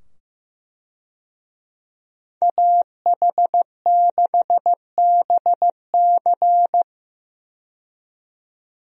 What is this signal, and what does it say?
Morse code, heard as a keyed tone, about 15 wpm, 700 Hz. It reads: AH6BC
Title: USA Call Signs - 5 Characters - 15wpm